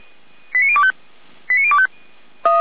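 Five-tone selective call (ZVEI), heard as quick runs of five tones, tones of about 70 ms. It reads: 89025 89025
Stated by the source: Five-tone ZVEI selective call received on an SDR.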